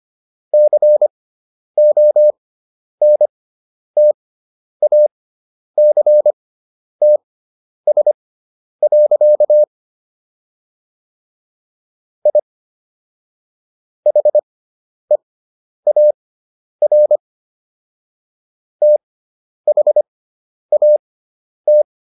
Morse code, heard as a keyed tone, about 25 wpm, spaced as for 10 wpm, 600 Hz. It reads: CONTACTS. I HEAR THAT